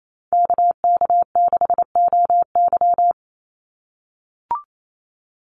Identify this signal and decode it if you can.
Morse code, heard as a keyed tone, about 28 wpm, 700 Hz. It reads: KK6OY
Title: USA Call Signs 28wpm